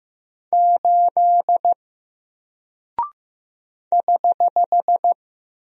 Morse code, heard as a keyed tone, about 15 wpm, 700 Hz. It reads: 8 <HH>
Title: Single Character (Letter:Number:Punctuation:Prosign) Alt 15wpm